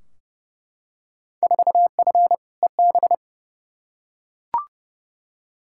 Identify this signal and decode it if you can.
Morse code, heard as a keyed tone, about 30 wpm, 700 Hz. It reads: 4F EB
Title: Field Day Exchanges - 30wpm